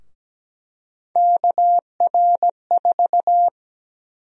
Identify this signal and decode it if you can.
Morse code, heard as a keyed tone, about 17 wpm, 700 Hz. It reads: KR4
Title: USA Call Sign Prefixes - Repeated 2x - 17wpm